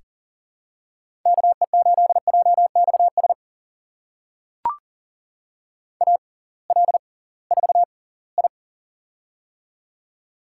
Morse code, heard as a keyed tone, about 40 wpm, spaced as for 23 wpm, 700 Hz. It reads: KE8JXS AL4I
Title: USA Call Signs - 6x Character Spacing - 40wpm